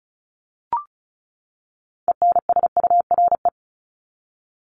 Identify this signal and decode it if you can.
Morse code, heard as a keyed tone, about 34 wpm, 700 Hz. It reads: ENSURE
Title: Top 1001-1250 Words 35wpm